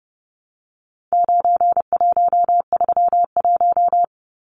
Morse code, heard as a keyed tone, about 30 wpm, 700 Hz. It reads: 9131
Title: Numbers - 4-Digit 30wpm